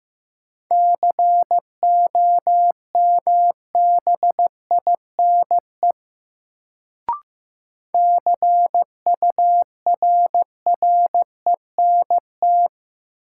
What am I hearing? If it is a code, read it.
Morse code, heard as a keyed tone, about 15 wpm, 700 Hz. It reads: COMBINE CURRENT
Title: Sentences from Top 1600 Words - New Word Drill Alt - 15wpm